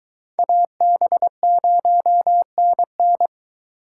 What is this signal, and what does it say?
Morse code, heard as a keyed tone, about 22 wpm, 700 Hz. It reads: AB0NN